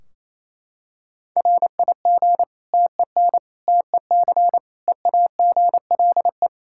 Morse code, heard as a keyed tone, about 28 wpm, 700 Hz. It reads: RIG TEN TEC EAGLE
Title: Practice QSO - Exchange 2 - Info - 28wpm